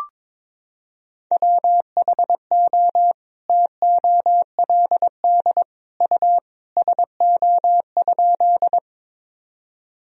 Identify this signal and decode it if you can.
Morse code, heard as a keyed tone, about 22 wpm, 700 Hz. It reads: WHO TOLD U SO?
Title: Sentences from Top 400 Words and Abbreviations 22wpm